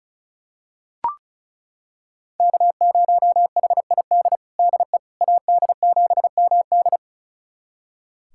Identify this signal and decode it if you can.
Morse code, heard as a keyed tone, about 36 wpm, 700 Hz. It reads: K0HID DE AD7MD
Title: USA Callsign DE Callsign - 5 Characters - 35wpm